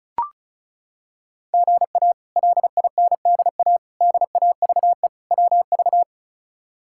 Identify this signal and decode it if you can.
Morse code, heard as a keyed tone, about 36 wpm, 700 Hz. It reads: GA LINDA DAVE WV
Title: K1USN SST QSO Practice - 35wpm